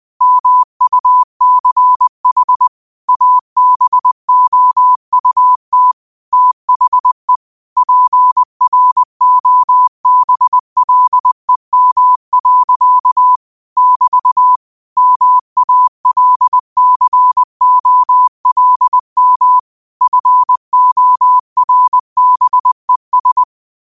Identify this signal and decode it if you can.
Morse code, heard as a keyed tone, about 20 wpm, 1000 Hz. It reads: MUCH ABOUT THE PROBLEM. <BT> MALCOLM FORBES
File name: qotdfiles / 20wpm